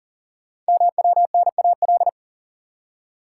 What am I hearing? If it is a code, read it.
Morse code, heard as a keyed tone, about 39 wpm, 690 Hz. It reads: MWNAL
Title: Code Groups - 5 Letters - 40wpm